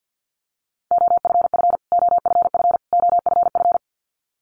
Morse code, heard as a keyed tone, about 50 wpm, 700 Hz. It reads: OFF OFF OFF
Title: Top 500 Words Repeated 50wpm